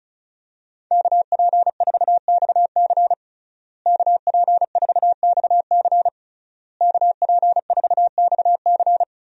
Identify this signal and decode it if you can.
Morse code, heard as a keyed tone, about 36 wpm, 700 Hz. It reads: KP4XC KP4XC KP4XC
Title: DXCC-100 Call Signs - 5 Characters - Repeated 3x - 35wpm